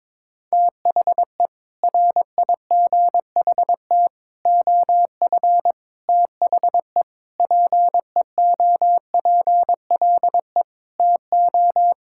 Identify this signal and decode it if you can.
Morse code, heard as a keyed tone, about 22 wpm, 700 Hz. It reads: THE RIGHT OF THE PEOPLE TO
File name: practice / 22wpm / Sets of 6 Words - 22wpm